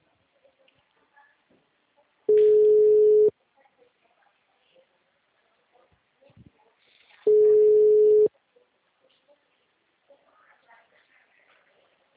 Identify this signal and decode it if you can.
Ringback tone (Europe, 425 Hz), with speech.